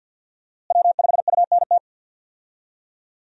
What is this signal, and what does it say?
Morse code, heard as a keyed tone, about 47 wpm, 700 Hz. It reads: W5UNT
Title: USA Call Signs - 5 Characters - Repeated 3x - 50wpm